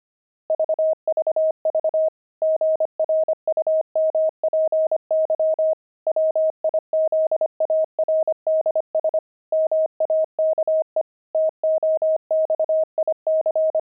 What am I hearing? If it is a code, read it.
Morse code, heard as a keyed tone, about 25 wpm, 630 Hz. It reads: VVV GRUMPY WIZARDS MAKE TOXIC
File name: pangram-fs-25-cs-25-grumpy